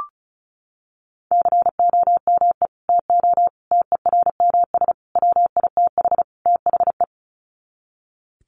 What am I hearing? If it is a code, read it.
Morse code, heard as a keyed tone, about 34 wpm, 700 Hz. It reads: COME TO TERMS WITH THE
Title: Sets of 5 Words Encore2 35wpm